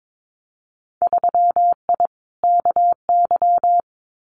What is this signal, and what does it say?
Morse code, heard as a keyed tone, about 22 wpm, 700 Hz. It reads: 3I KY
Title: Winter Field Day Exchanges 22wpm